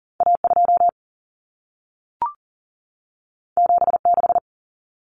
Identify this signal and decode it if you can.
Morse code, heard as a keyed tone, about 40 wpm, 700 Hz. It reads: A2 76
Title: Common Cut Numbers - 2-Digit 40wpm